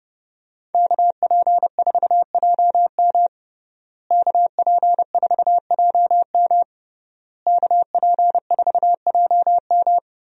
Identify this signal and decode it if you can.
Morse code, heard as a keyed tone, about 32 wpm, 700 Hz. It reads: KP4JM KP4JM KP4JM
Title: DXCC-100 Call Signs - 5 Characters - Repeated 3x - 30wpm